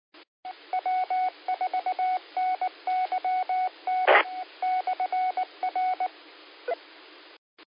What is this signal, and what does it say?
Morse code, heard as a keyed tone, about 19 wpm, spaced as for 17 wpm, 740 Hz. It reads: W4NYT/R
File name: qsorec SimplexLogic 18-12-52